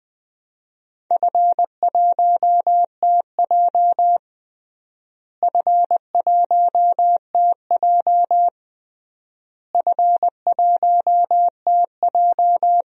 Morse code, heard as a keyed tone, about 20 wpm, 700 Hz. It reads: F1TJ F1TJ F1TJ
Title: DXCC-100 Call Signs - 4 Characters - Repeated 3x - 20wpm